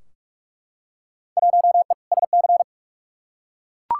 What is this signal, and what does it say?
Morse code, heard as a keyed tone, about 43 wpm, 700 Hz. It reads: 1E SC
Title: Field Day Exchanges - 45wpm